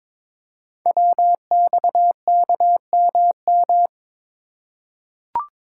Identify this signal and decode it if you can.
Morse code, heard as a keyed tone, about 22 wpm, 700 Hz. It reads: WXKMM
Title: Code Groups - 5 Letters - 22wpm